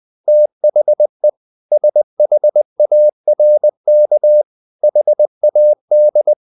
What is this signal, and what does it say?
Morse code, heard as a keyed tone, about 20 wpm, 600 Hz. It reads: THE SHARK HAD
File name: the old man and the sea part 2 20wpm